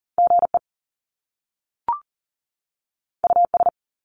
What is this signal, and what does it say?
Morse code, heard as a keyed tone, about 41 wpm, 700 Hz. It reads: GE US
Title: Top Two Letter Combinations in Words 40wpm